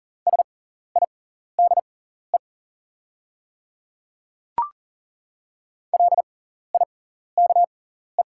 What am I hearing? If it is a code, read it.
Morse code, heard as a keyed tone, about 40 wpm, spaced as for 14 wpm, 690 Hz. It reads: SIDE LIKE
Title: 6x Character Spacing - Top 500 Words - 40wpm